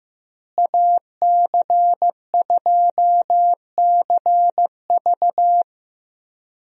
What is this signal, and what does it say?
Morse code, heard as a keyed tone, about 16 wpm, 700 Hz. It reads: AC2CV
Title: USA Call Signs - 5 Characters - Repeated 3x - 15wpm